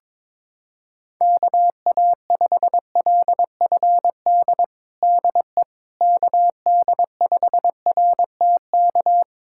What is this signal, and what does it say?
Morse code, heard as a keyed tone, about 22 wpm, 700 Hz. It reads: KA5LFD DE KD5RTK
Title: USA Callsign DE Callsign - 6 Characters - 22wpm